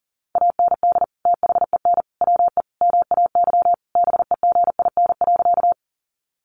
Morse code, heard as a keyed tone, about 40 wpm, 700 Hz. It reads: AND THEN WE MAY BEGIN.
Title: Sentences from Top 300 Words 40wpm